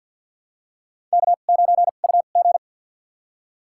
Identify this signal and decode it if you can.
Morse code, heard as a keyed tone, about 52 wpm, spaced as for 37 wpm, 700 Hz. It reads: K9UG
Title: USA Call Signs - 4 Characters - 2x Character Spacing - 50wpm